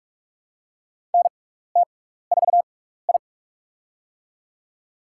Morse code, heard as a keyed tone, about 44 wpm, spaced as for 15 wpm, 700 Hz. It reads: NT4I
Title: USA Call Signs - 4 Characters - 6x Character Spacing - 45wpm